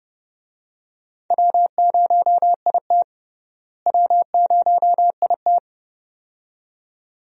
Morse code, heard as a keyed tone, about 30 wpm, 700 Hz. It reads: W0IT W0IT
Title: USA Call Signs - 4 Characters - Repeated 2x - 30wpm